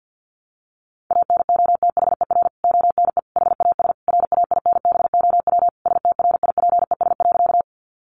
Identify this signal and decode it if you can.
Morse code, heard as a keyed tone, about 49 wpm, 700 Hz. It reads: ANOTHER ONE HAS RAINBOW STRIPES.
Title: Additional Easy Phrases 50wpm